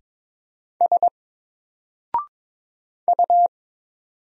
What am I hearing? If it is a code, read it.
Morse code, heard as a keyed tone, about 22 wpm, 700 Hz. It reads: S U
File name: Single-Letter-Number-22wpm